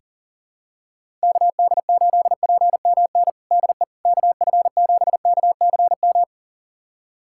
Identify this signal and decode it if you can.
Morse code, heard as a keyed tone, about 40 wpm, 700 Hz. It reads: KD8PMN DE KF7KCM
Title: USA Callsign DE Callsign - 6 Characters - 40wpm